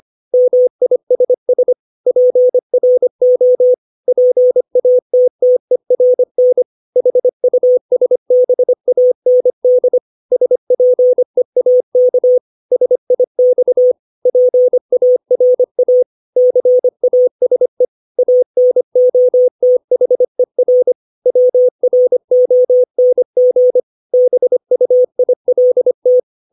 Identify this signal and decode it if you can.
Morse code, heard as a keyed tone, about 25 wpm, 500 Hz. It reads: MISS PRO PATTERN HUSBAND SPEAK SIX PARA CASE ANOTHER WRONG BUILT
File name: 632 words / 632 Words 25WPM